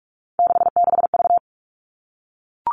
Morse code, heard as a keyed tone, about 44 wpm, 700 Hz. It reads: 66V
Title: All Cut Numbers - 3-Digit 45wpm